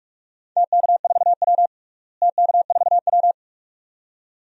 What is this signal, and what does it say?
Morse code, heard as a keyed tone, about 46 wpm, 700 Hz. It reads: TK4W TK4W
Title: DXCC-100 Call Signs - 4 Characters - Repeated 2x - 45wpm